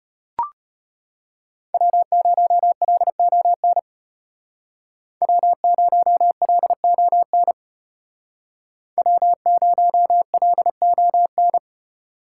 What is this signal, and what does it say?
Morse code, heard as a keyed tone, about 32 wpm, 700 Hz. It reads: W0LON W0LON W0LON